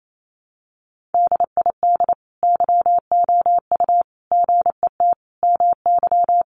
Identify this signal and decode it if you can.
Morse code, heard as a keyed tone, about 28 wpm, 700 Hz. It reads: DID YOU GET MY
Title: Sentences from Top 1100 Words 28wpm